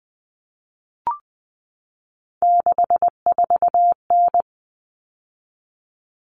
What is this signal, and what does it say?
Morse code, heard as a keyed tone, about 20 wpm, 700 Hz. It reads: 64N
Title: All Cut Numbers - 3-Digit 20wpm